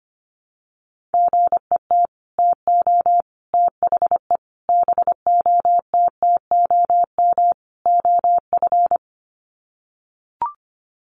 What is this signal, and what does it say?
Morse code, heard as a keyed tone, about 25 wpm, 700 Hz. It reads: GET TO THE BOTTOM OF
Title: Sets of 5 Words Encore 25wpm